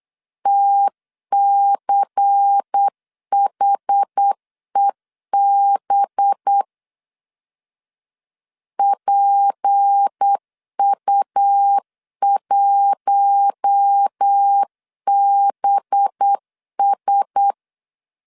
Morse code, heard as a keyed tone, about 8 wpm, 800 Hz. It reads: TCHEB PU1BS
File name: LEZIONI / 6 PKX0 / 6 P 8wpm 3025